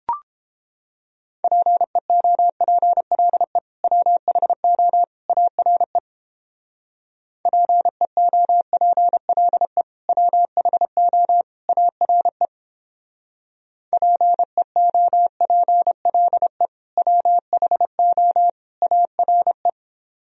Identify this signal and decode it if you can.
Morse code, heard as a keyed tone, about 29 wpm, 700 Hz. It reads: PEOPLE WHO ARE PEOPLE WHO ARE PEOPLE WHO ARE